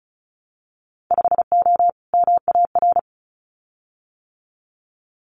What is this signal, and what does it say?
Morse code, heard as a keyed tone, about 36 wpm, 700 Hz. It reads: 5O MAR
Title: Winter Field Day Exchanges Encore 35wpm